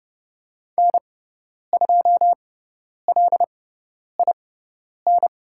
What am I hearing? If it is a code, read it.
Morse code, heard as a keyed tone, about 30 wpm, spaced as for 10 wpm, 700 Hz. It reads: N2LIN
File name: Lesson 16g - Letter c - Call Signs 30wpm